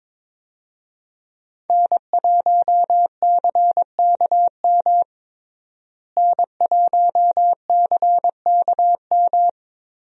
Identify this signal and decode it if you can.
Morse code, heard as a keyed tone, about 22 wpm, 700 Hz. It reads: N1CKM N1CKM
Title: USA Call Signs - 5 Characters - Repeated 3x - 22wpm